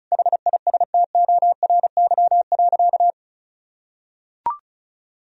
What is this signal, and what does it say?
Morse code, heard as a keyed tone, about 36 wpm, 700 Hz. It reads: HISTORY.
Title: Sentences from Top 1000 Words - Review Part 4 - 35wpm